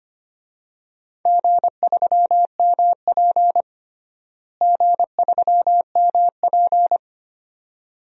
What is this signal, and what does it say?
Morse code, heard as a keyed tone, about 25 wpm, 700 Hz. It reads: G3MP G3MP
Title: DXCC-100 Call Signs - 4 Characters - Repeated 2x - 25wpm